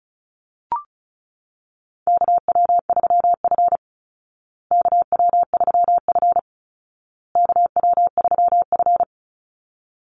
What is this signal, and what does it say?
Morse code, heard as a keyed tone, about 34 wpm, 700 Hz. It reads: KW3F KW3F KW3F
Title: Call Signs Repeated 2x - 35wpm